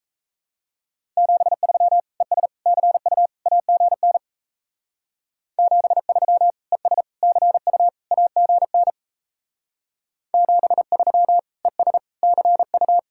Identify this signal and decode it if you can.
Morse code, heard as a keyed tone, about 35 wpm, 700 Hz. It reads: 73 ES CU AGN 73 ES CU AGN 73 ES CU
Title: Speed-Racing Common QSO Phrases - 28wpm